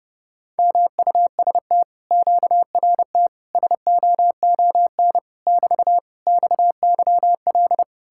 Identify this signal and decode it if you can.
Morse code, heard as a keyed tone, about 33 wpm, 700 Hz. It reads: MUST QRT SOON <BT> XYL